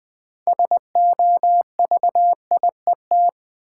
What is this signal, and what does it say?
Morse code, heard as a keyed tone, about 21 wpm, 700 Hz. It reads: SOVIET
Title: Sets of 5 Words Encore - New Word Drill - 20wpm